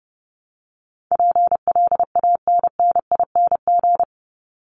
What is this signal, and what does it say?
Morse code, heard as a keyed tone, about 30 wpm, 700 Hz. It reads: PLANNING
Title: ING Suffix - Top 100 Words with Base Words 30wpm